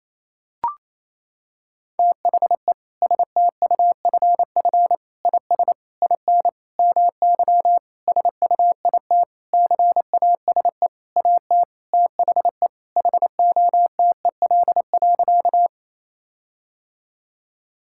Morse code, heard as a keyed tone, about 28 wpm, 700 Hz. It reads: THE STUFF IS IN MY SUIT CASE AT THE HOTEL.
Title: Sentences from Top 1600 Words 28wpm